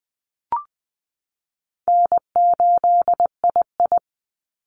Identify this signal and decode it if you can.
Morse code, heard as a keyed tone, about 20 wpm, 700 Hz. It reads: N8II
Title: Easy Call Signs Repeated 2x - 20wpm